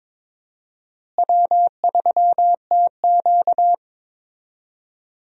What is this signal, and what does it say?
Morse code, heard as a keyed tone, about 21 wpm, 700 Hz. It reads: W3TQ
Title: USA Call Signs - 4 Characters - Repeated 2x - 22wpm